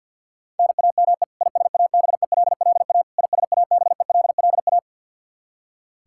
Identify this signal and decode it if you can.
Morse code, heard as a keyed tone, about 52 wpm, 700 Hz. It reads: NAME ISABELLA ISABELLA
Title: Practice QSO - Exchange 1 - Beginning - 50wpm